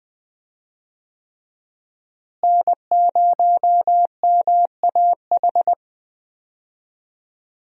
Morse code, heard as a keyed tone, about 20 wpm, 700 Hz. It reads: N0MAH
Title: USA Call Signs - 5 Characters - Repeated 3x - 20wpm